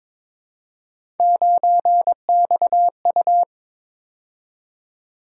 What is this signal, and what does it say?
Morse code, heard as a keyed tone, about 21 wpm, 700 Hz. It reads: 9XU
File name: practice / 22wpm / Three Letter-Number Combinations - 22wpm